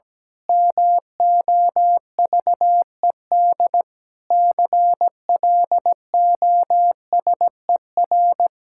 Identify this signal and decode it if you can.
Morse code, heard as a keyed tone, about 17 wpm, 700 Hz. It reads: MOVED CLOSER
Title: Pangram Challenge v2 - 17wpm